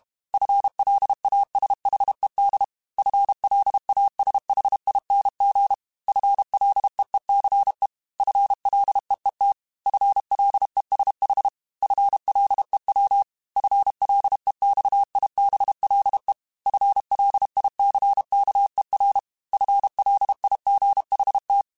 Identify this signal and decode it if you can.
Morse code, heard as a keyed tone, about 32 wpm, 800 Hz. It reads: FLASHED FLASHING FLEECE FLEET FLESH FLEW FLEXIBLE FLICKER FLIGHT
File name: FL Runs / FL Runs mp3s / FL Runs @30